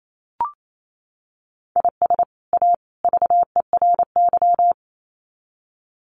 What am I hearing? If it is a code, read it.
Morse code, heard as a keyed tone, about 28 wpm, 700 Hz. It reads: IS A VERY
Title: Sets of 3 Words 28wpm